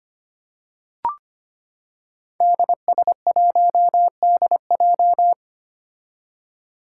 Morse code, heard as a keyed tone, about 25 wpm, 700 Hz. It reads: DS1DJ
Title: Five Challenging Letter-Number Combinations 25wpm